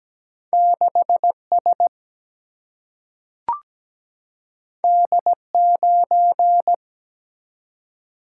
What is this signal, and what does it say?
Morse code, heard as a keyed tone, about 17 wpm, 700 Hz. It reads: 6 S D 9
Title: Single-Letter Single-Number 17wpm